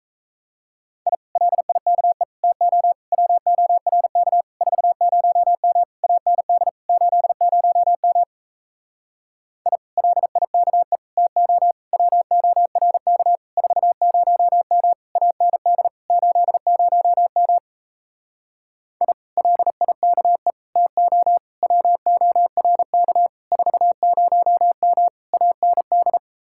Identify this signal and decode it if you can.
Morse code, heard as a keyed tone, about 38 wpm, 700 Hz. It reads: I LIKE TO WORK 40M AND 80M I LIKE TO WORK 40M AND 80M I LIKE TO WORK 40M AND